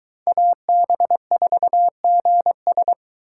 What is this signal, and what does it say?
Morse code, heard as a keyed tone, about 23 wpm, 700 Hz. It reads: AB4GS